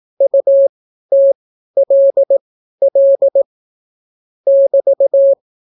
Morse code, heard as a keyed tone, about 18 wpm, spaced as for 12 wpm, 550 Hz. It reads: UTLL <BT>